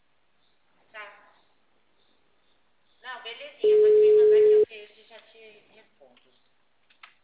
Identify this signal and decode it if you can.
Ringback tone (Europe, 425 Hz), with speech.
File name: out- -2015- - - .20869